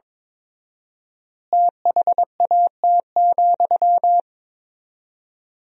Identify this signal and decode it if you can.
Morse code, heard as a keyed tone, about 22 wpm, 700 Hz. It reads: THAT,
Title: 4x Word-Spacing - Sentences from Top 500 Words - 22wpm